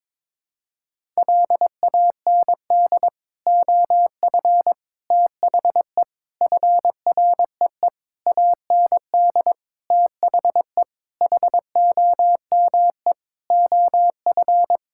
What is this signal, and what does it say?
Morse code, heard as a keyed tone, about 22 wpm, 700 Hz. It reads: LAND OF THE FREE AND THE HOME OF
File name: Sets of 9 Words - 22wpm